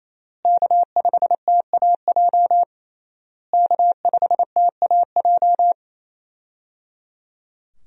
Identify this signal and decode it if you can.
Morse code, heard as a keyed tone, about 28 wpm, 700 Hz. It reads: K5TAJ K5TAJ
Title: USA Call Signs - 5 Characters - Repeated 3x - 28wpm